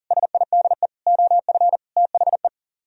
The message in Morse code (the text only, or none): SIDE OF THE